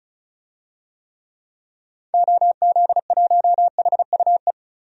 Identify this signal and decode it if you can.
Morse code, heard as a keyed tone, about 36 wpm, 700 Hz. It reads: OZ1HUE
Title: DXCC-100 Call Signs - 6 Characters - 35wpm